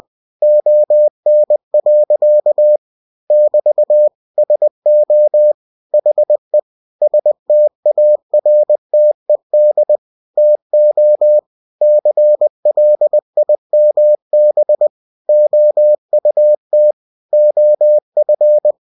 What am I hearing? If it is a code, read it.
Morse code, heard as a keyed tone, about 20 wpm, 600 Hz. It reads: ON. <BT> SO HE STARTED TO CLIMB OUT OF
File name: cw-books / winnie-the-pooh / winnie-20wpm-page-0016